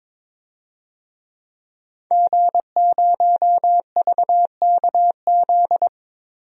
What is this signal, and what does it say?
Morse code, heard as a keyed tone, about 22 wpm, 700 Hz. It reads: G0VKZ
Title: DXCC-100 Call Signs - 5 Characters - Repeated 2x - 22wpm